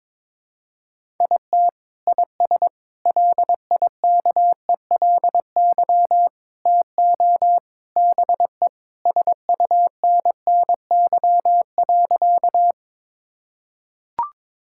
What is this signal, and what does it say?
Morse code, heard as a keyed tone, about 22 wpm, 700 Hz. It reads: IT IS LIKELY TO BE SUNNY.
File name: practice / 22wpm / Sentences with 3-Word Phrases - 22wpm